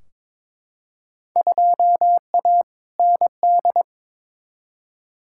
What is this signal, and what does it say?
Morse code, heard as a keyed tone, about 22 wpm, 700 Hz. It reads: 2A ND
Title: Field Day Exchanges Encore - 22wpm